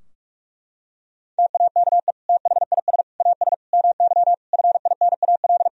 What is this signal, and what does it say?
Morse code, heard as a keyed tone, about 42 wpm, 700 Hz. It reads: TAKE THIS AS MY FINAL
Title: Sentences from Top 1000 Words - Review Part 2 - 45wpm